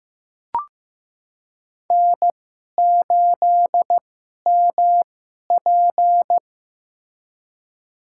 Morse code, heard as a keyed tone, about 15 wpm, spaced as for 11 wpm, 700 Hz. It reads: N8MP
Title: USA Call Signs - 4 Characters - 2x Character Spacing - 15wpm